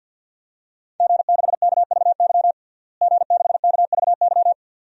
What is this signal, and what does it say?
Morse code, heard as a keyed tone, about 51 wpm, 700 Hz. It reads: G6KVY G6KVY
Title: DXCC-100 Call Signs - 5 Characters - Repeated 2x - 50wpm